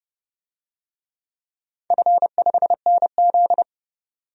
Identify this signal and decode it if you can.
Morse code, heard as a keyed tone, about 31 wpm, 700 Hz. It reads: F5NZ
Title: DXCC-100 Call Signs - 4 Characters - 30wpm